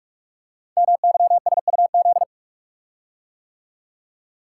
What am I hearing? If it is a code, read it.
Morse code, heard as a keyed tone, about 44 wpm, 700 Hz. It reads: MYSUZ
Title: Code Groups - 5 Letters - 45wpm